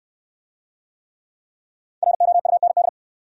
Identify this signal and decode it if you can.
Morse code, heard as a keyed tone, about 68 wpm, 700 Hz. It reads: ROUND